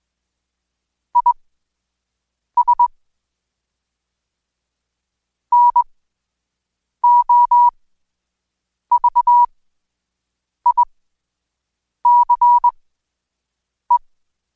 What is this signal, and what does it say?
Morse code, heard as a keyed tone, about 21 wpm, spaced as for 6 wpm, 970 Hz. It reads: IS NOVICE
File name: Code Quick/AudioDownld2/Seg109-Day14-DayDrill 14